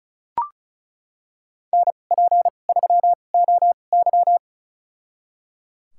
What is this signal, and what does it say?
Morse code, heard as a keyed tone, about 36 wpm, spaced as for 25 wpm, 700 Hz. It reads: NP3OY
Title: USA Call Signs - 2x Character Spacing - 35wpm